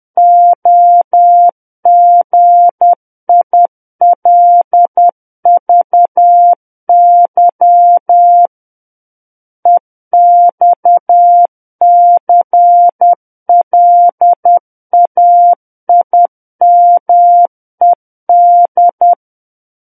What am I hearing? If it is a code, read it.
Morse code, heard as a keyed tone, about 10 wpm, 700 Hz. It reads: OGILVY EXCLAIMED
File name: Book One / 01-Chapter 1 - 10 WPM